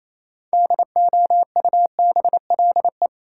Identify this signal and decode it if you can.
Morse code, heard as a keyed tone, about 28 wpm, 700 Hz. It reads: DOUBLE
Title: Sentences from Top 1400 Words - New Word Drill Alt - 28wpm